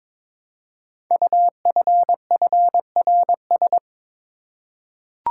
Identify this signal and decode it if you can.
Morse code, heard as a keyed tone, about 22 wpm, 700 Hz. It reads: UFFRS
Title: Code Groups - 5 Letters - 22wpm